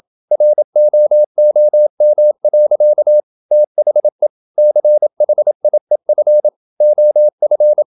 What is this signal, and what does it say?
Morse code, heard as a keyed tone, about 27 wpm, 600 Hz. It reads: ROOM. THE CHIEF OF
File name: Never 0027wpm